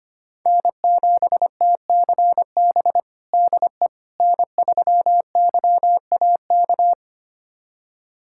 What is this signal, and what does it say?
Morse code, heard as a keyed tone, about 25 wpm, 700 Hz. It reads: N7TCB DE N3YAK
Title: USA Callsign DE Callsign - 5 Characters - 25wpm